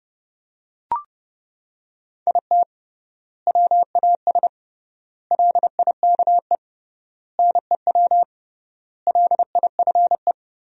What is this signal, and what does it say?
Morse code, heard as a keyed tone, about 32 wpm, 700 Hz. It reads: IT WAS LIKE NEW LIFE
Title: 3x Word-Spacing - Sentences from Top 300 Words - 30wpm